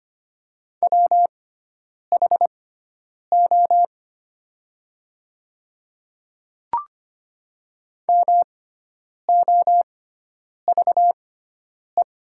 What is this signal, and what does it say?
Morse code, heard as a keyed tone, about 25 wpm, spaced as for 9 wpm, 690 Hz. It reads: WHO MOVE
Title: 6x Character Spacing - Top 200 Words - 25wpm